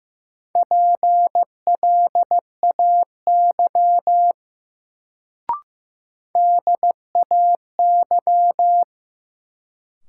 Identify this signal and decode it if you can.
Morse code, heard as a keyed tone, about 15 wpm, 700 Hz. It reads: PLAY DAY
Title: Sentences from Top 200 Words - New Word Drill - 15wpm